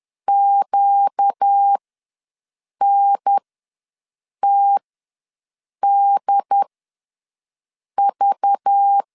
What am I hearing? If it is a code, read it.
Morse code, heard as a keyed tone, about 10 wpm, spaced as for 6 wpm, 800 Hz. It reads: QNTDV
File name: LEZIONI / 5 LBVC / 5 LBV ws 10wpm 3645